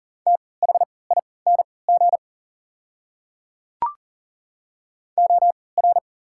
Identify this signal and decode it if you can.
Morse code, heard as a keyed tone, about 40 wpm, spaced as for 23 wpm, 690 Hz. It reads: THING OR
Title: 3x Character Spacing - Top 100 Words - 40wpm